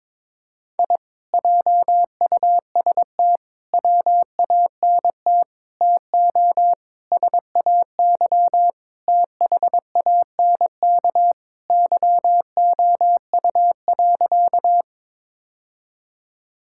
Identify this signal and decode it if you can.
Morse code, heard as a keyed tone, about 22 wpm, 700 Hz. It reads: I JUST WANT TO SAY THANK YOU.
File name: practice / 22wpm / Sentences with 2-Word Phrases - 22wpm